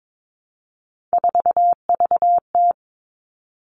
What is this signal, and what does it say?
Morse code, heard as a keyed tone, about 22 wpm, 700 Hz. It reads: 4VT
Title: All Cut Numbers - 3-Digit 22wpm